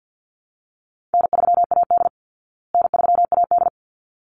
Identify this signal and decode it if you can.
Morse code, heard as a keyed tone, about 50 wpm, 700 Hz. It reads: N3AD N3AD
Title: Easy Call Signs Repeated 2x - 50wpm